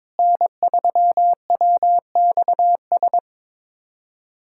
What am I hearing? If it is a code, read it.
Morse code, heard as a keyed tone, about 22 wpm, 700 Hz. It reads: N3WXS
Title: USA Callsign DE Callsign - 5 Characters - 22wpm